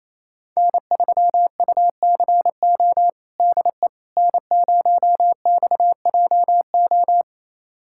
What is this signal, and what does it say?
Morse code, heard as a keyed tone, about 28 wpm, 700 Hz. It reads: N3UCO DE N0XJO
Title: USA Callsign DE Callsign - 5 Characters - 28wpm